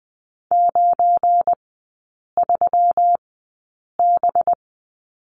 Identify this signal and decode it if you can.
Morse code, heard as a keyed tone, about 20 wpm, spaced as for 8 wpm, 700 Hz. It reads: 93B
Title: 184 Lesson 30e - Letter b - Three Characters 20wpm